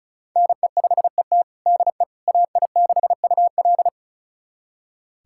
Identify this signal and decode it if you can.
Morse code, heard as a keyed tone, about 36 wpm, 700 Hz. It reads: NE5ET DE AI6UL
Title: USA Callsign DE Callsign - 5 Characters - 35wpm